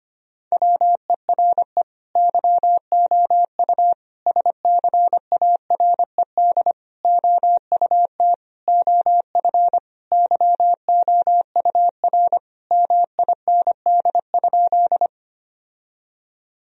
Morse code, heard as a keyed tone, about 25 wpm, 700 Hz. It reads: WERE YOU SCARED OUT OF YOUR MIND?
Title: Sentences from Top 1000 Words - Review Part 4 - 25wpm